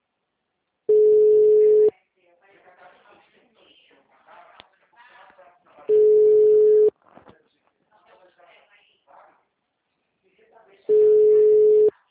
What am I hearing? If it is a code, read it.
Ringback tone (Europe, 425 Hz), with speech.
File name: out- -2044- - - .36820